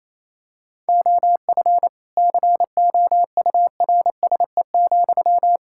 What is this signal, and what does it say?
Morse code, heard as a keyed tone, about 28 wpm, 700 Hz. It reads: OF COURSE,
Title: Sentences from Top 1000 Words - Review Part 6 - 28wpm